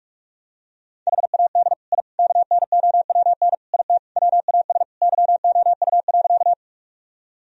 Morse code, heard as a keyed tone, about 46 wpm, 700 Hz. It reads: HAD I KNOWN IT WAS YOU.
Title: Sentences from Top 1000 Words - Review Part 3 - 45wpm